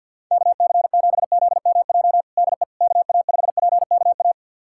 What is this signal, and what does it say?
Morse code, heard as a keyed tone, about 51 wpm, 700 Hz. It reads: KC7ZMJ DE KA5PKA
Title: USA Callsign DE Callsign - 6 Characters - 50wpm